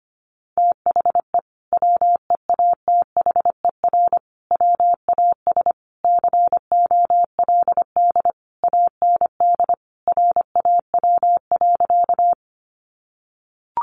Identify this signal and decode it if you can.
Morse code, heard as a keyed tone, about 25 wpm, 700 Hz. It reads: THE WEATHER WAS COLD AND RAW.
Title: Sentences from Top 1500 Words 25wpm